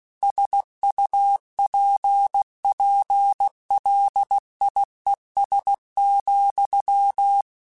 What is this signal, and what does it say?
Morse code, heard as a keyed tone, about 16 wpm, 800 Hz. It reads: SUPPLIES,
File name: Technical Subjects / Technical Subjects-mp3s / Sw Pwr Sup @15